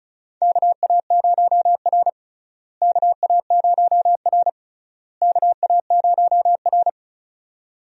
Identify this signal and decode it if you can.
Morse code, heard as a keyed tone, about 34 wpm, 700 Hz. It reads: KA0R KA0R KA0R
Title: USA Call Signs - 4 Characters - Repeated 3x - 35wpm